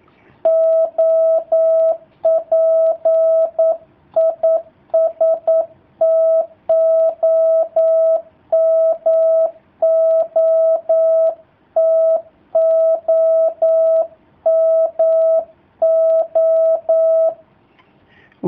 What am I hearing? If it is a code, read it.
Morse code, heard as a keyed tone, about 9 wpm, 650 Hz. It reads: OP IS TOMO TOMO
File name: jh8cbh / 2019cw / sp09- w